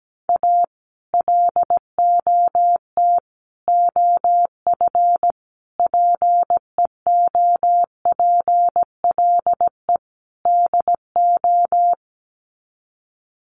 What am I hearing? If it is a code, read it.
Morse code, heard as a keyed tone, about 17 wpm, 700 Hz. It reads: A LOT OF PEOPLE DO
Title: Sets of 5 Words 17wpm